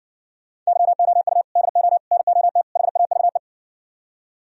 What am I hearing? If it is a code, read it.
Morse code, heard as a keyed tone, about 56 wpm, 700 Hz. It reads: YOU DO NOT HAVE